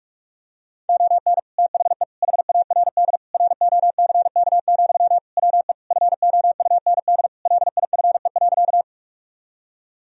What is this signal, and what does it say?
Morse code, heard as a keyed tone, about 48 wpm, 700 Hz. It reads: ON THE HARD ROCK, WE FOUND LIFE.
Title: Sentences from Top 1000 Words - Review Part 2 - 45wpm